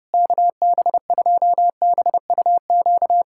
Morse code, heard as a keyed tone, about 30 wpm, 700 Hz. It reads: KB2BUQ
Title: USA Callsign DE Callsign - 6 Characters - 30wpm